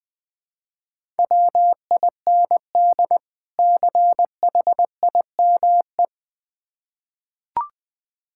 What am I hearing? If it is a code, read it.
Morse code, heard as a keyed tone, about 20 wpm, 700 Hz. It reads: WIND CHIME
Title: Difficult Words with Hints - 20wpm